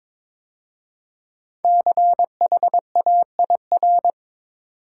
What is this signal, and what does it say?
Morse code, heard as a keyed tone, about 22 wpm, 700 Hz. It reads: CHAIR
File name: 5 Letter Words - First 3 Letters Given - 22wpm